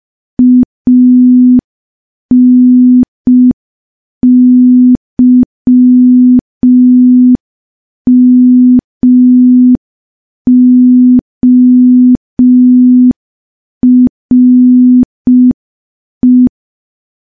Morse code, heard as a keyed tone, about 5 wpm, 250 Hz. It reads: ANYMORE
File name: morse / morse-training-05WPM